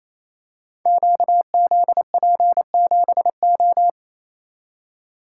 Morse code, heard as a keyed tone, about 28 wpm, 700 Hz. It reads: QZP7O
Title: Code Groups - 5 Letter:Numbers - 28wpm